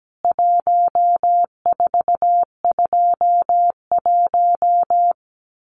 Morse code, heard as a keyed tone, about 17 wpm, 700 Hz. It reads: 1421
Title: CWT Exchange 17wpm